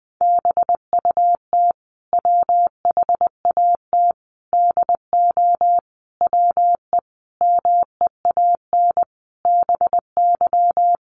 Morse code, heard as a keyed tone, about 20 wpm, 700 Hz. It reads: BUT WHAT DO WE MEAN BY
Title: Sentences from Top 1100 Words 20wpm